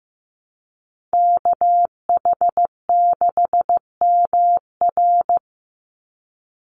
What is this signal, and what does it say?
Morse code, heard as a keyed tone, about 15 wpm, 700 Hz. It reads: KH6MR
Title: USA Call Signs 15wpm